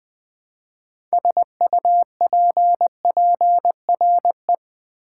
Morse code, heard as a keyed tone, about 20 wpm, 700 Hz. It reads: SUPPRE
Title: Build A Word - 3 Letters At A Time - 8-Letter Words - 20wpm